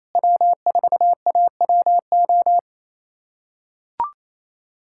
Morse code, heard as a keyed tone, about 27 wpm, 700 Hz. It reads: W4AWO
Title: USA Callsign DE Callsign - 5 Characters - 28wpm